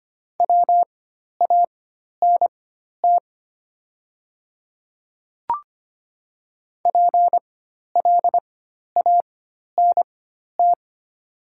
Morse code, heard as a keyed tone, about 25 wpm, spaced as for 12 wpm, 700 Hz. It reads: WANT PLANT
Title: 4x Character Spacing - Top 300 Words - 25wpm